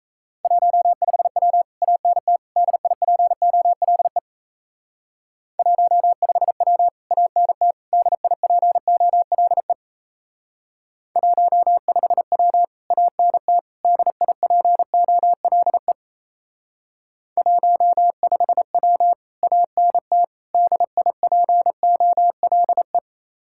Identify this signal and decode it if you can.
Morse code, heard as a keyed tone, about 34 wpm, 700 Hz. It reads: 15W ANT DIPOLE 15W ANT DIPOLE 15W ANT DIPOLE 15W ANT DIPOLE
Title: Speed-Racing Common QSO Phrases Encore - 28wpm